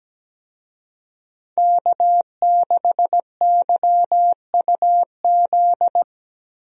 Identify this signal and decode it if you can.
Morse code, heard as a keyed tone, about 17 wpm, 700 Hz. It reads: K6YUZ
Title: USA Call Signs - 5 Characters - Repeated 2x - 17wpm